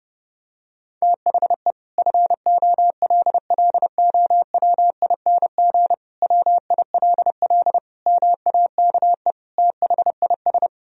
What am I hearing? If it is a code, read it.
Morse code, heard as a keyed tone, about 30 wpm, 700 Hz. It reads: THE FOLLOWING WILL MAKE THIS
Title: Sentences from Top 1000 Words - Review Part 4 - 30wpm